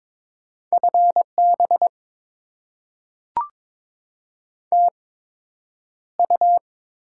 Morse code, heard as a keyed tone, about 22 wpm, spaced as for 9 wpm, 700 Hz. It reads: FB TU
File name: Top 20 QSO Elements - 8x Character Spacing - 22wpm